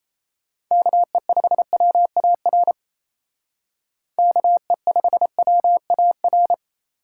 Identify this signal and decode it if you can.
Morse code, heard as a keyed tone, about 36 wpm, 700 Hz. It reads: KE5WAR KE5WAR